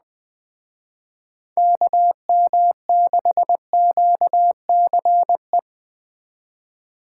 Morse code, heard as a keyed tone, about 21 wpm, 700 Hz. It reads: KM6QCE
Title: Speed Racing - USA 2x3 Call Signs 17wpm